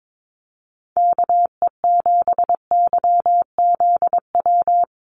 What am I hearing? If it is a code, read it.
Morse code, heard as a keyed tone, about 22 wpm, 700 Hz. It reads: KE7YZW
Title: USA Call Signs 22wpm